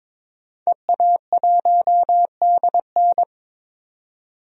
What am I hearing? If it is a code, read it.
Morse code, heard as a keyed tone, about 22 wpm, 700 Hz. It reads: EA1DN
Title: DXCC-100 Call Signs - 5 Characters - 22wpm